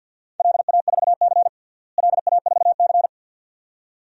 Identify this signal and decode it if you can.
Morse code, heard as a keyed tone, about 51 wpm, 700 Hz. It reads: LA4C LA4C
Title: DXCC-100 Call Signs - 4 Characters - Repeated 2x - 50wpm